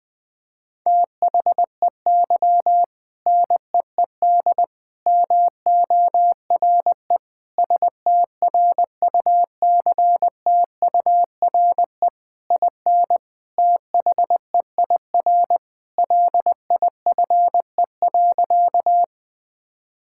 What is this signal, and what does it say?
Morse code, heard as a keyed tone, about 21 wpm, 700 Hz. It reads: THEY NEED MORE STRUCTURE IN THEIR LIFE.
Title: Sentences from Top 1000 Words - Review Part 6 - 20wpm